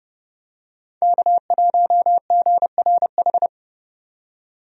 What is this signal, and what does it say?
Morse code, heard as a keyed tone, about 30 wpm, 700 Hz. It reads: K1GRH
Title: USA Call Signs - 5 Characters - 30wpm